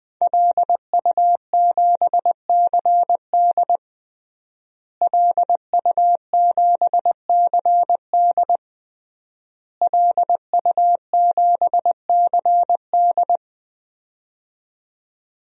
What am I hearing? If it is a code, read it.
Morse code, heard as a keyed tone, about 20 wpm, 700 Hz. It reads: LU7CD LU7CD LU7CD
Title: DXCC-100 Call Signs - 5 Characters - Repeated 3x - 20wpm